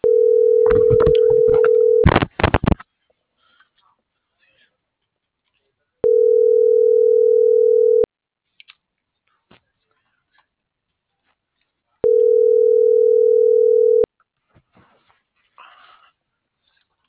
Ringback tone (North America), 440 + 480 Hz.